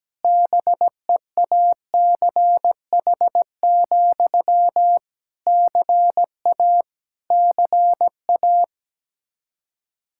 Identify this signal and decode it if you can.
Morse code, heard as a keyed tone, about 17 wpm, 700 Hz. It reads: BEACH, CA CA
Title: Practice QSO - Exchange 1 - Beginning - 17wpm